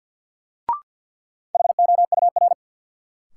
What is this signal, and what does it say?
Morse code, heard as a keyed tone, about 46 wpm, 700 Hz. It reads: HOUR
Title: Sentences from Top 400 Words - New Word Drill - 50wpm